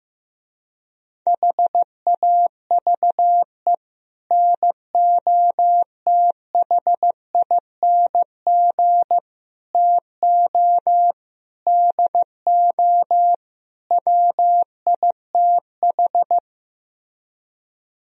Morse code, heard as a keyed tone, about 15 wpm, 700 Hz. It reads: HAVE NOTHING TO DO WITH
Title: Speed-Racing - Sets of 5 Words 15wpm